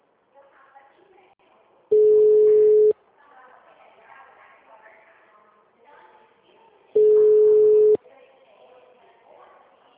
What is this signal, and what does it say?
Ringback tone (Europe, 425 Hz), with speech.